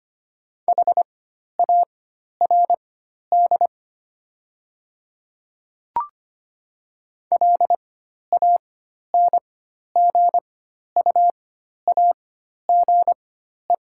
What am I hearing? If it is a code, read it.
Morse code, heard as a keyed tone, about 25 wpm, spaced as for 12 wpm, 700 Hz. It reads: HARD LANGUAGE
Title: 4x Character Spacing - Top 500 Words - 25wpm